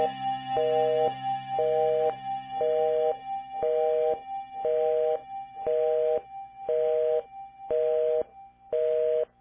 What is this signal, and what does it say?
Busy tone (North America), 480 + 620 Hz, with noise.
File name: bellcore-radio-05